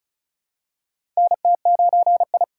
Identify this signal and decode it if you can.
Morse code, heard as a keyed tone, about 34 wpm, 700 Hz. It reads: NT9I
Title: USA Call Signs - 2x Character Spacing - 35wpm